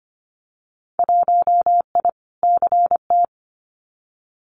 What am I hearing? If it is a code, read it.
Morse code, heard as a keyed tone, about 25 wpm, 700 Hz. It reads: 1I CT
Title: Winter Field Day Exchanges 25wpm